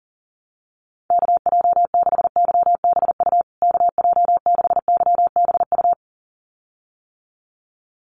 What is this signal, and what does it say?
Morse code, heard as a keyed tone, about 40 wpm, 700 Hz. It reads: KJ6YBU KJ6YBU
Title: USA Call Signs Repeated 2x 40wpm